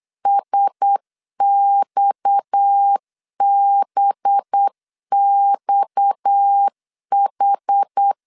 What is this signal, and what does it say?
Morse code, heard as a keyed tone, about 8 wpm, 800 Hz. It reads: SXBXH
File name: LEZIONI / 6 PKX0 / 6 PKX 8wpm 3041